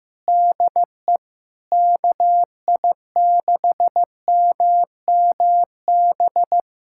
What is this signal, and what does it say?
Morse code, heard as a keyed tone, about 15 wpm, 700 Hz. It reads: DE KI6MMB
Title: USA Callsign DE Callsign 15wpm